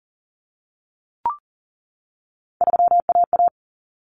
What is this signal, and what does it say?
Morse code, heard as a keyed tone, about 40 wpm, 700 Hz. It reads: 3AA
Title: Common Cut Numbers - 3-Digit 40wpm